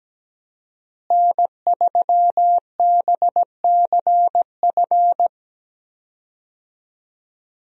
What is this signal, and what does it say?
Morse code, heard as a keyed tone, about 17 wpm, 700 Hz. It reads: N3BCF
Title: USA Call Signs - 5 Characters - Repeated 3x - 17wpm